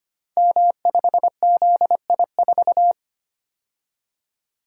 Morse code, heard as a keyed tone, about 25 wpm, 700 Hz. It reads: M5ZI4
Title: Code Groups - 5 Letter:Numbers - 25wpm